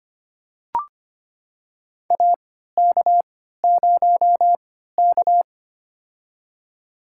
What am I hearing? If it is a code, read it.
Morse code, heard as a keyed tone, about 25 wpm, spaced as for 14 wpm, 700 Hz. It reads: AK0K
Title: USA Call Signs - 4 Characters - 3x Character Spacing - 25wpm